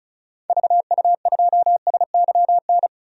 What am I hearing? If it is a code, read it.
Morse code, heard as a keyed tone, about 36 wpm, 700 Hz. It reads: VU2SYN
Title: DXCC-100 Call Signs - 6 Characters - 35wpm